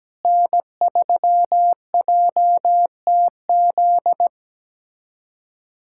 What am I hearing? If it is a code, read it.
Morse code, heard as a keyed tone, about 17 wpm, 700 Hz. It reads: N3JTZ
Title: USA Callsign DE Callsign - 5 Characters - 17wpm